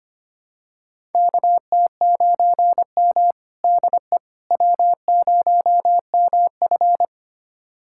Morse code, heard as a keyed tone, about 25 wpm, 700 Hz. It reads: KT9M DE W0MF
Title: USA Callsign DE Callsign - 4 Characters - 25wpm